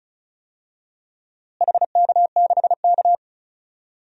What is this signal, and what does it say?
Morse code, heard as a keyed tone, about 36 wpm, 700 Hz. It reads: HK6K
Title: DXCC-100 Call Signs - 4 Characters - Repeated 2x - 35wpm